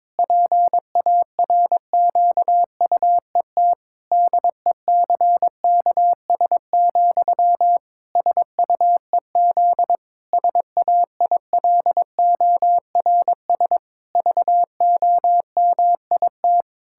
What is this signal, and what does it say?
Morse code, heard as a keyed tone, about 22 wpm, 700 Hz. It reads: PARQUET DECKS, SUEZ SAILORS VOMIT
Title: Pangram Challenge v2 - 22wpm